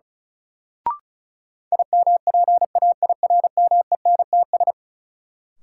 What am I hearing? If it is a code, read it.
Morse code, heard as a keyed tone, about 36 wpm, 700 Hz. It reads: IMPAIRMENTS
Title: Sets of 5 Words Encore - New Word Drill - 35wpm